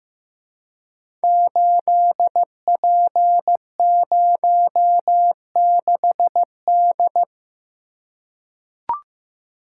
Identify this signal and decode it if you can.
Morse code, heard as a keyed tone, about 15 wpm, 700 Hz. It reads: 8P06D
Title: Code Groups - 5 Letter:Numbers - 15wpm